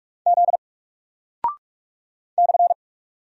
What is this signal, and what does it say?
Morse code, heard as a keyed tone, about 46 wpm, 700 Hz. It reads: Z /
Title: Single Character (Letter:Number:Punctuation:Prosign) Alt 45wpm